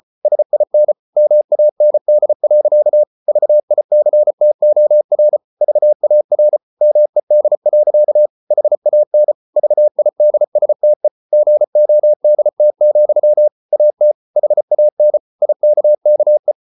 Morse code, heard as a keyed tone, about 34 wpm, 600 Hz. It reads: SIN MAND. VICTOR VAR MED. HAN VIDSTE GODT, AT HAN IKKE
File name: Ferie QRQ 34wpm